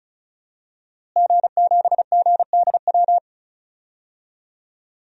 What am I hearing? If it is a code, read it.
Morse code, heard as a keyed tone, about 36 wpm, 700 Hz. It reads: G7GDW
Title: DXCC-100 Call Signs - 5 Characters - Repeated 2x - 35wpm